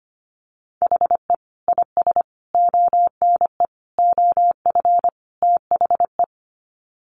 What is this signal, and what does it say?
Morse code, heard as a keyed tone, about 25 wpm, 690 Hz. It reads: HE IS ONE OF THE
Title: Sets of 5 Words Encore 25wpm